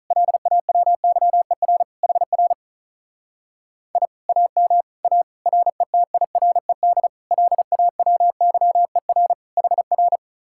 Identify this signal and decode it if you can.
Morse code, heard as a keyed tone, about 44 wpm, 700 Hz. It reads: LAWYER HR I AM A RETIRED LAWYER HR